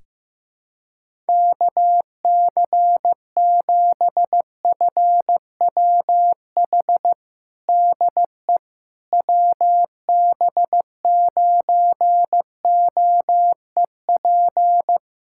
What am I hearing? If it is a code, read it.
Morse code, heard as a keyed tone, about 16 wpm, 700 Hz. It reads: KC7FWH DE WB9OEP
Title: USA Callsign DE Callsign - 6 Characters - 15wpm